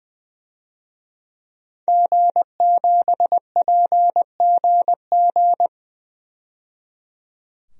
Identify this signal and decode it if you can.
Morse code, heard as a keyed tone, about 20 wpm, 700 Hz. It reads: G7PGG
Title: DXCC-100 Call Signs - 5 Characters - 20wpm